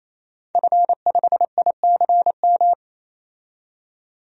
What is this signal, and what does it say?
Morse code, heard as a keyed tone, about 28 wpm, 700 Hz. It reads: F5ICM
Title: DXCC-100 Call Signs - 5 Characters - 28wpm